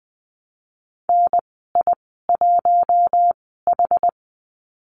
Morse code, heard as a keyed tone, about 20 wpm, spaced as for 14 wpm, 700 Hz. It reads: NI1H
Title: 062 Lesson 11g - Letter h - Call Signs 20wpm